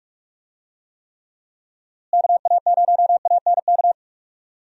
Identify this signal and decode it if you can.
Morse code, heard as a keyed tone, about 46 wpm, 700 Hz. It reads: KA0ANK
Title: USA Call Signs - 6 Characters - Repeated 3x - 45wpm